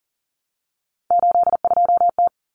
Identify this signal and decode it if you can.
Morse code, heard as a keyed tone, about 40 wpm, 700 Hz. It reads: 82T
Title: Common Cut Numbers - 3-Digit 40wpm